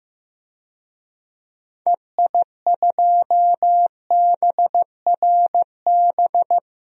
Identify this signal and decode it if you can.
Morse code, heard as a keyed tone, about 15 wpm, 700 Hz. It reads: EI2BRB
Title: DXCC-100 Call Signs - 6 Characters - 15wpm